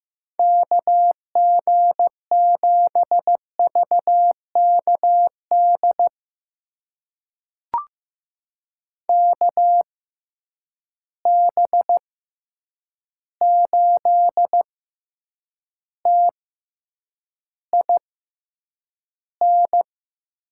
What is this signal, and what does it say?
Morse code, heard as a keyed tone, about 15 wpm, spaced as for 8 wpm, 700 Hz. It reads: KG7VKD KB8TIN
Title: USA Call Signs - 6 Characters - 6x Character Spacing - 15wpm